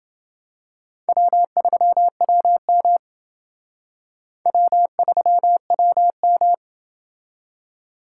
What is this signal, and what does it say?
Morse code, heard as a keyed tone, about 28 wpm, 700 Hz. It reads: W3WM W3WM